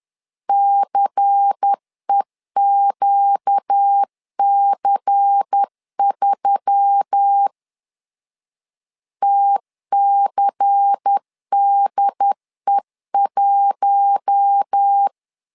Morse code, heard as a keyed tone, about 10 wpm, 800 Hz. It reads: CEQC3 TCDE1
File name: LEZIONI / 5 LBVC / 5 LBVC 10wpm 2349